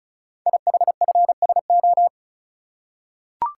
Morse code, heard as a keyed tone, about 36 wpm, 700 Hz. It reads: IHFSO
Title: Code Groups - 5 Letters - 35wpm